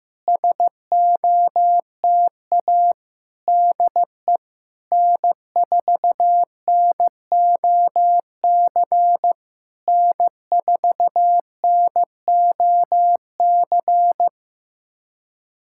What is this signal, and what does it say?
Morse code, heard as a keyed tone, about 15 wpm, 700 Hz. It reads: SOTA DE N4NOC N4NOC
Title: SOTA QSO Practice 15wpm